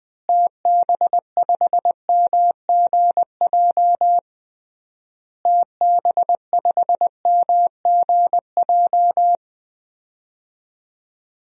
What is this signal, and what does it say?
Morse code, heard as a keyed tone, about 21 wpm, 700 Hz. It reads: TB5MGJ TB5MGJ
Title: DXCC-100 Call Signs - 6 Characters - Repeated 3x - 20wpm